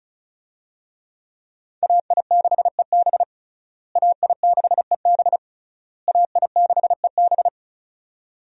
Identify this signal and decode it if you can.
Morse code, heard as a keyed tone, about 36 wpm, 700 Hz. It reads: AI6EB AI6EB AI6EB
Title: USA Call Signs - 5 Characters - Repeated 3x - 35wpm